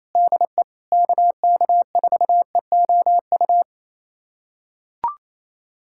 Morse code, heard as a keyed tone, about 30 wpm, 700 Hz. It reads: DE KK4EOU
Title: USA Callsign DE Callsign 28wpm